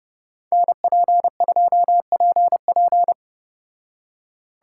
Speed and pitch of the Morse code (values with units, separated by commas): 31 wpm, 700 Hz